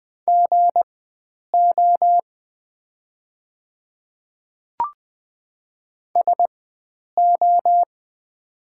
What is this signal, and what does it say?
Morse code, heard as a keyed tone, about 20 wpm, spaced as for 9 wpm, 700 Hz. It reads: GO SO
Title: 4x Character Spacing - Top 100 Words - 20wpm